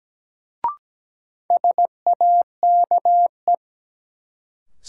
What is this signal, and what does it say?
Morse code, heard as a keyed tone, about 17 wpm, 700 Hz. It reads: SAKE
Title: Sets of 4 Words - New Word Drill - 17wpm